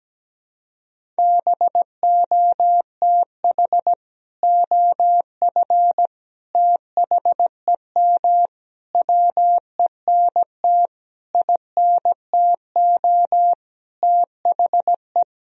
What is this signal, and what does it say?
Morse code, heard as a keyed tone, about 17 wpm, 700 Hz. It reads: BOTH OF THEM WENT INTO THE
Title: Sentences from Top 1000 Words - Review Part 1 - 17wpm